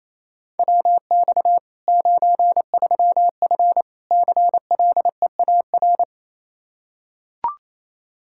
Morse code, heard as a keyed tone, about 28 wpm, 700 Hz. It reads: WX 93F CLEAR
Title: Practice QSO - Exchange 2 - Info - 28wpm